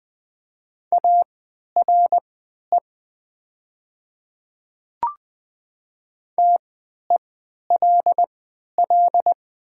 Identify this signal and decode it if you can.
Morse code, heard as a keyed tone, about 20 wpm, spaced as for 11 wpm, 700 Hz. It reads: ARE TELL
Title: 3x Character Spacing - Top 300 Words - 20wpm